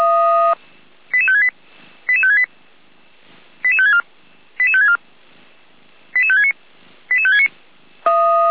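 Five-tone selective call (ZVEI), heard as quick runs of five tones, tones of about 70 ms. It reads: 88568 88568 88564 88564 88569 88569
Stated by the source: ZVEI five-tone selective call 88355 received by SDR.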